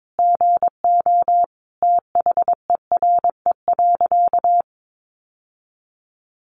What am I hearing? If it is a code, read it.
Morse code, heard as a keyed tone, about 22 wpm, 700 Hz. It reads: GO THERE.
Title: Sentences from Top 800 Words 22wpm